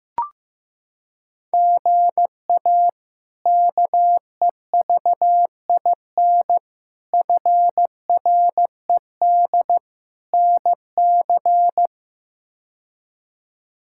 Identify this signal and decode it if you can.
Morse code, heard as a keyed tone, about 16 wpm, 700 Hz. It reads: GA KEVIN FRED NC
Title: K1USN SST QSO Practice - 15wpm